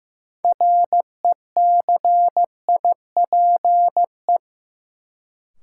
Morse code, heard as a keyed tone, about 15 wpm, 700 Hz. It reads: RECIPE
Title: Sentences from Top 1200 Words - New Word Drill Alt - 15wpm